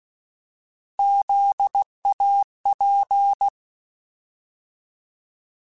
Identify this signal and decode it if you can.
Morse code, heard as a keyed tone, about 16 wpm, 800 Hz. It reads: ZAP
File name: Alphabetical / Alphabetical mp3s / Zs @15